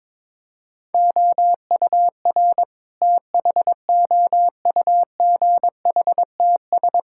Morse code, heard as a keyed tone, about 22 wpm, 700 Hz. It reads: OUR THOUGHTS
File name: practice / 22wpm / Sentences with 8-Word Phrases - 22wpm